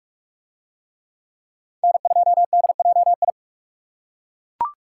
Morse code, heard as a keyed tone, about 41 wpm, 690 Hz. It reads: N2DJI